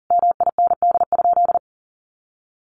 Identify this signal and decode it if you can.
Morse code, heard as a keyed tone, about 40 wpm, 700 Hz. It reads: MIND?
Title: Sentences from Top 1400 Words 40wpm